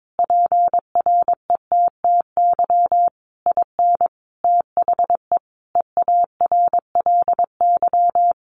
Morse code, heard as a keyed tone, about 22 wpm, 700 Hz. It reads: PRETTY IN THE EARLY
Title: Sentences from Top 1300 Words 22wpm